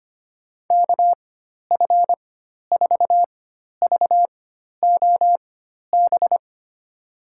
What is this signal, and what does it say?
Morse code, heard as a keyed tone, about 25 wpm, spaced as for 12 wpm, 700 Hz. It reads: KF4VOB
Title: USA Call Signs - 4x Character Spacing - 25wpm